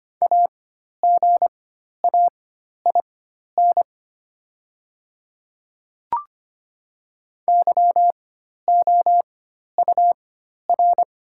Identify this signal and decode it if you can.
Morse code, heard as a keyed tone, about 25 wpm, spaced as for 12 wpm, 700 Hz. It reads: AGAIN YOUR
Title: 4x Character Spacing - Top 200 Words - 25wpm